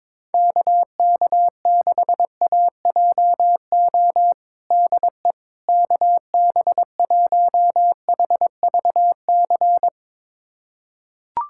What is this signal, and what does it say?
Morse code, heard as a keyed tone, about 22 wpm, 700 Hz. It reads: KK6AJO DE KB1HVC
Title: USA Callsign DE Callsign 22wpm